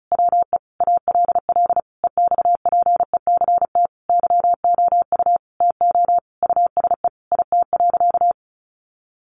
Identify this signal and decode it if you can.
Morse code, heard as a keyed tone, about 34 wpm, 700 Hz. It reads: WE ALL EXPECT YOU TO USE IT.
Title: Sentences from Top 1100 Words 35wpm